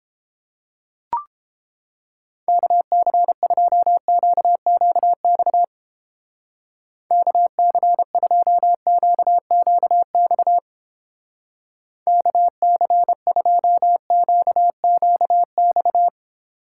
Morse code, heard as a keyed tone, about 29 wpm, 700 Hz. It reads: KC2QQX KC2QQX KC2QQX